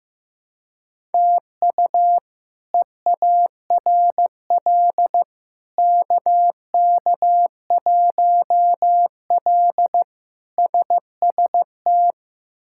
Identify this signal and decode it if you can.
Morse code, heard as a keyed tone, about 15 wpm, 700 Hz. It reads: TU EARL KK1L SST
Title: K1USN SST QSO Practice - 15wpm